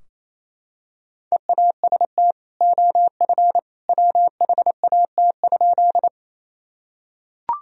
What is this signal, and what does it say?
Morse code, heard as a keyed tone, about 27 wpm, 700 Hz. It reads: EAST OF WHAT?
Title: Sentences from Top 1000 Words - Review Part 3 - 28wpm